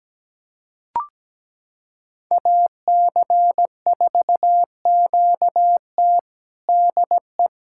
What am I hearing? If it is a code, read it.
Morse code, heard as a keyed tone, about 17 wpm, 700 Hz. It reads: AC4QT DE
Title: USA Callsign DE Callsign - 5 Characters - 17wpm